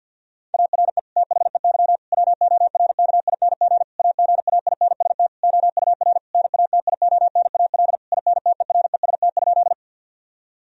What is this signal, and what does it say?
Morse code, heard as a keyed tone, about 50 wpm, 700 Hz. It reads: ARE THEY WORKING AGAINST OUR NATIONAL INTEREST?
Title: Sentences from Top 1000 Words - Review Part 4 - 50wpm